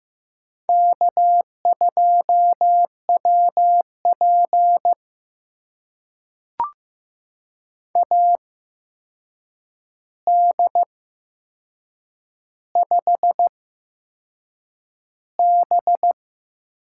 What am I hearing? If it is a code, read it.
Morse code, heard as a keyed tone, about 15 wpm, spaced as for 6 wpm, 700 Hz. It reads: K2WP AD5B
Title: USA Call Signs - 4 Characters - 8x Character Spacing - 15wpm